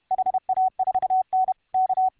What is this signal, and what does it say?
Morse code, heard as a keyed tone, about 31 wpm, 750 Hz. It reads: HA4N K